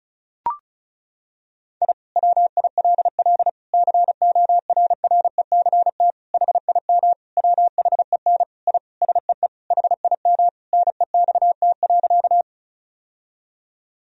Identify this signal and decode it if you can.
Morse code, heard as a keyed tone, about 36 wpm, 700 Hz. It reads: I WILL CORRECT HIM WHEN I SEE HIM NEXT.
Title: Sentences from Top 1000 Words - Review Part 2 - 35wpm